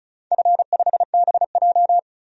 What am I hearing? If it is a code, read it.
Morse code, heard as a keyed tone, about 36 wpm, 700 Hz. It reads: F5BJ
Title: DXCC-100 Call Signs - 4 Characters - 35wpm